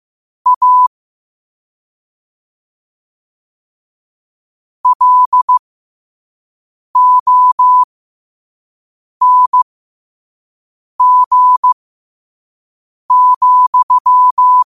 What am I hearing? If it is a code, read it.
Morse code, heard as a keyed tone, about 15 wpm, spaced as for 5 wpm, 1000 Hz. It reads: A LONG,
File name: qotdfiles / 5wpm